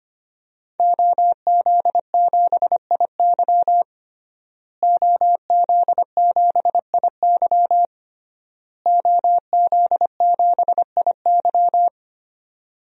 Morse code, heard as a keyed tone, about 25 wpm, 700 Hz. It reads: OZ7IY OZ7IY OZ7IY
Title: DXCC-100 Call Signs - 5 Characters - Repeated 3x - 25wpm